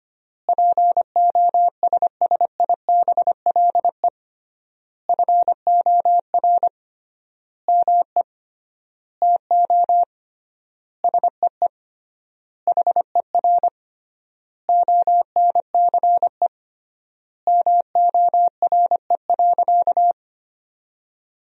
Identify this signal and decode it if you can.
Morse code, heard as a keyed tone, about 25 wpm, 700 Hz. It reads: POSSIBLE FOR ME TO SEE HER ONCE MORE.
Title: 3x Word-Spacing - Sentences from Top 500 Words - 25wpm